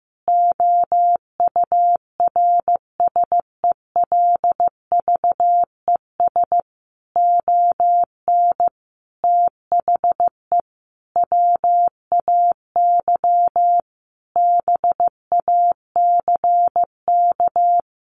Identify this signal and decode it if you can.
Morse code, heard as a keyed tone, about 15 wpm, 700 Hz. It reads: OURSELVES ON THE WAY BACK
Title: A Sound Of Thunder 15wpm